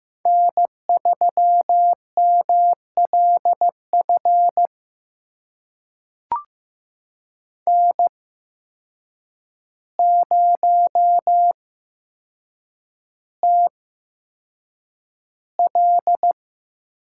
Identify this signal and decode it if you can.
Morse code, heard as a keyed tone, about 15 wpm, spaced as for 7 wpm, 700 Hz. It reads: N3MLF N0TL
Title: USA Call Signs - 5 Characters - 8x Character Spacing - 15wpm